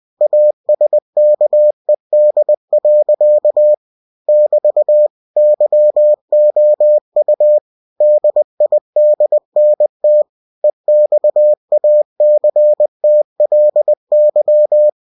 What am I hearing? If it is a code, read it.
Morse code, heard as a keyed tone, about 20 wpm, 600 Hz. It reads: ASKED. <BT> YOU DIDNT EXACTLY